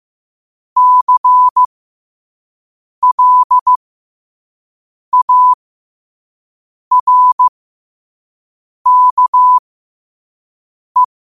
Morse code, heard as a keyed tone, about 15 wpm, spaced as for 5 wpm, 1000 Hz. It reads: CLARKE